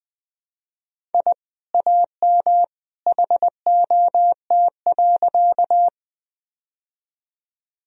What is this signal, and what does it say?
Morse code, heard as a keyed tone, about 20 wpm, 700 Hz. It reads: I AM HOT.
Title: Sentences from Top 1000 Words - Review Part 3 - 20wpm